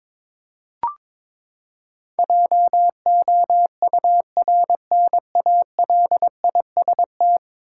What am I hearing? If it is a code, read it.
Morse code, heard as a keyed tone, about 23 wpm, 700 Hz. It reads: JOURNALIST
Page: morse-code-ninja-practice-22wpm